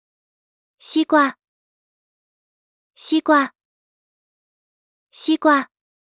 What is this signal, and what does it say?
Speech. There is no code.